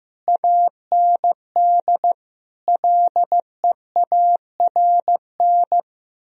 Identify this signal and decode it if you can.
Morse code, heard as a keyed tone, about 15 wpm, 700 Hz. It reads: AND LEARN
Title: Sentences from Top 1000 Words - Review Part 4 - 15wpm